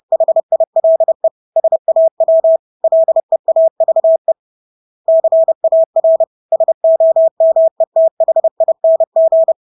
Morse code, heard as a keyed tone, about 30 wpm, 650 Hz. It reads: HILE SAW LEAVE CAR SOMETHING
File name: Engelske ord 30-30 wpm